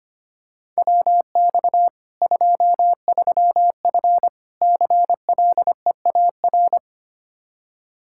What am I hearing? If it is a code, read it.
Morse code, heard as a keyed tone, about 25 wpm, 700 Hz. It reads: WX 23F CLEAR
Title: K1USN SST QSO Practice 25wpm